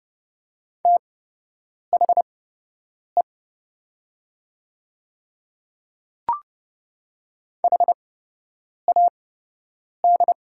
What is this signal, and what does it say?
Morse code, heard as a keyed tone, about 31 wpm, spaced as for 8 wpm, 700 Hz. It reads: THE HAD
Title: 8x Character Spacing - Top 100 Words - 30wpm